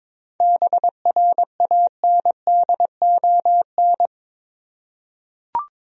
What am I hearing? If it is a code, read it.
Morse code, heard as a keyed tone, about 22 wpm, 700 Hz. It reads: BRANDON
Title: USA Common Names - 22wpm